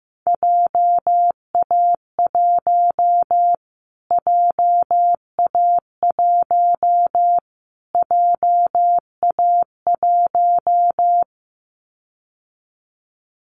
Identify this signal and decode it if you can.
Morse code, heard as a keyed tone, about 15 wpm, 700 Hz. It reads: JA1 JA1 JA1
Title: Callsign Prefixes for Top 25 DXCC - Repeated 3x - 15wpm